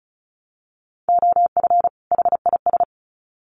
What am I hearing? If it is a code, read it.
Morse code, heard as a keyed tone, about 34 wpm, 700 Hz. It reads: OF HIS
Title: Sets of 2 Words 35wpm